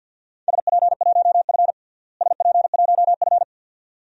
Morse code, heard as a keyed tone, about 47 wpm, 700 Hz. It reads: SP1F SP1F
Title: DXCC-100 Call Signs - 4 Characters - Repeated 2x - 50wpm